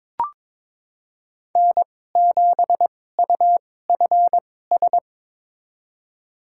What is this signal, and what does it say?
Morse code, heard as a keyed tone, about 22 wpm, spaced as for 16 wpm, 700 Hz. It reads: N7UFS
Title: USA Call Signs - 5 Characters - 2x Character Spacing - 22wpm